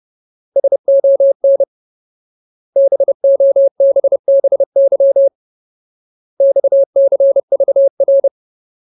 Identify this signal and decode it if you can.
Morse code, heard as a keyed tone, about 30 wpm, 550 Hz. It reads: SON BOBBY XCVR
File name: Ultimate Word List / 30WPM Ultimate